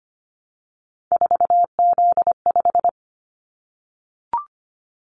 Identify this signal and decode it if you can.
Morse code, heard as a keyed tone, about 25 wpm, 700 Hz. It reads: 4Z5
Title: Callsign Prefixes for Top 50 DXCC - 25wpm